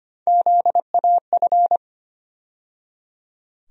Morse code, heard as a keyed tone, about 25 wpm, 700 Hz. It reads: ZAF
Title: USA Call Sign Postfixes - 3 Letters 25wpm